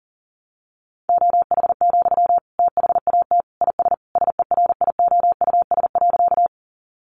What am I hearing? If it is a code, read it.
Morse code, heard as a keyed tone, about 37 wpm, 700 Hz. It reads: OH, THAT IS SERIOUS.
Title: Sentences from Top 1100 Words 40wpm